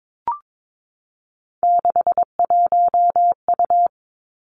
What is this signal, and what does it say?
Morse code, heard as a keyed tone, about 22 wpm, 700 Hz. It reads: 61U
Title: Three Challenging Letter-Number Combinations 22wpm